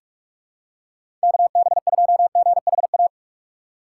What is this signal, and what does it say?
Morse code, heard as a keyed tone, about 45 wpm, 700 Hz. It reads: KB2GHA
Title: USA Call Signs - 6 Characters - 2x Character Spacing - 45wpm